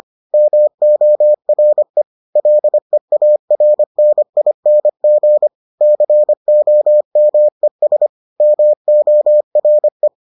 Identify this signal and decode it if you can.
Morse code, heard as a keyed tone, about 25 wpm, 600 Hz. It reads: MORE LEARNING COMES MORE